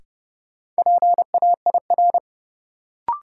Morse code, heard as a keyed tone, about 30 wpm, 700 Hz. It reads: PAIR
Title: Sentences from Top 1200 Words - New Word Drill Alt - 30wpm